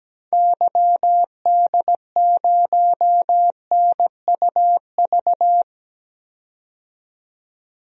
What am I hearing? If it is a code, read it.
Morse code, heard as a keyed tone, about 17 wpm, 700 Hz. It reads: YD0NUV
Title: DXCC-100 Call Signs - 6 Characters - 17wpm